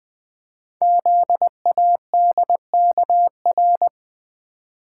Morse code, heard as a keyed tone, about 20 wpm, 700 Hz. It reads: ZADKR
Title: Code Groups - 5 Letters - 20wpm